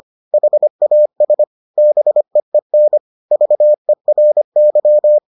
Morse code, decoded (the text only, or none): HAS BEEN VERY